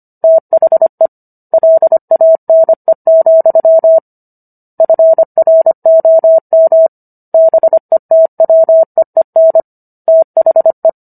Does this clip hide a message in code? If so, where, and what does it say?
Yes: Morse code from 0.2 to 10.9 s, heard as a keyed tone, about 25 wpm, 650 Hz. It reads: THE LANE, FROM BETWEEN THE